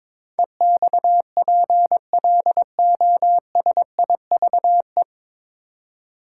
Morse code, heard as a keyed tone, about 22 wpm, 690 Hz. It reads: EXPLOSIVE
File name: 9 Letter Words - First 3 Letters Given - 22wpm